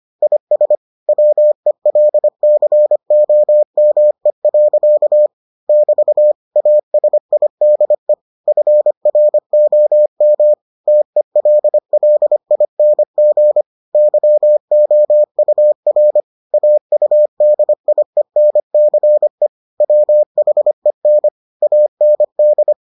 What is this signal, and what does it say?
Morse code, heard as a keyed tone, about 25 wpm, 600 Hz. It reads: IS WELCOME. <BT> ASIDE FROM TELLING YOUR AUDIENCE WHEN AND